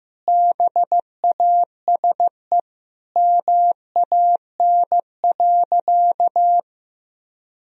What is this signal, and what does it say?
Morse code, heard as a keyed tone, about 15 wpm, 700 Hz. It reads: BASE MAN.
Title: Sentences from Top 1000 Words - Review Part 1 - 15wpm